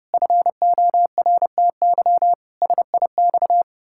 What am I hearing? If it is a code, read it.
Morse code, heard as a keyed tone, about 30 wpm, 700 Hz. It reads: FORTY SIX
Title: Pangram Challenge v2 - 30wpm